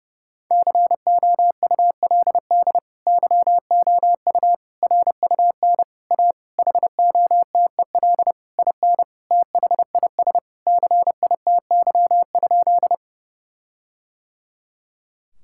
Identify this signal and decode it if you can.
Morse code, heard as a keyed tone, about 30 wpm, 700 Hz. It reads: COULD YOU RUN A HOTEL IN THIS CITY?
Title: Sentences from Top 1600 Words 30wpm